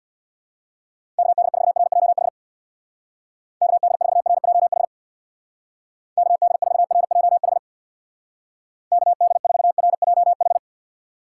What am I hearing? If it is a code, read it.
Morse code, heard as a keyed tone, about 62 wpm, 700 Hz. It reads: KD4RJH KD4RJH KD4RJH KD4RJH
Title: Speed Racing - USA 2x3 Call Signs 50wpm